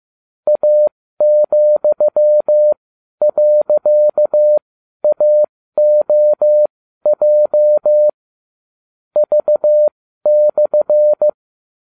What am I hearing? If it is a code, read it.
Morse code, heard as a keyed tone, about 15 wpm, spaced as for 12 wpm, 600 Hz. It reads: A,.AOJ V/